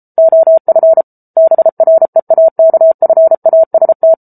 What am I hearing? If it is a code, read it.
Morse code, heard as a keyed tone, about 34 wpm, 650 Hz. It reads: OF BREAKFAST
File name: Book Two / 07-Chapter 7 - 33 WPM